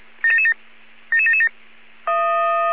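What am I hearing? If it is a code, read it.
Five-tone selective call (ZVEI), heard as quick runs of five tones, tones of about 70 ms. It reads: 77777
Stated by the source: ZVEI five-tone selective call received on an SDR.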